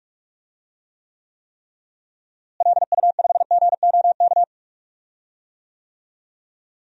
Morse code, heard as a keyed tone, about 41 wpm, 700 Hz. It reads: LU5GOK
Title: DXCC-100 Call Signs - 6 Characters - Repeated 3x - 45wpm